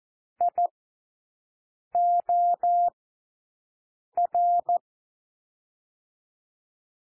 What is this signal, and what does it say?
Morse code, heard as a keyed tone, about 14 wpm, 700 Hz. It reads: I O R
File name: cwmp3 / W1 SOUND WORDS 14 wpm 0